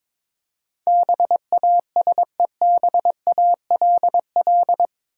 Morse code, heard as a keyed tone, about 22 wpm, 700 Hz. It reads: BASEBALL
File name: practice / 22wpm / Eight Letter Words - 22wpm